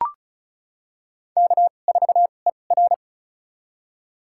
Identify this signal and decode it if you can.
Morse code, heard as a keyed tone, about 34 wpm, spaced as for 25 wpm, 690 Hz. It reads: K4ER
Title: USA Call Signs - 4 Characters - 2x Character Spacing - 35wpm